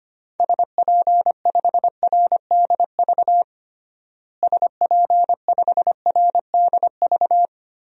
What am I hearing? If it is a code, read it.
Morse code, heard as a keyed tone, about 25 wpm, 700 Hz. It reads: SP5RDV SP5RDV
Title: DXCC-100 Call Signs - 6 Characters - Repeated 3x - 25wpm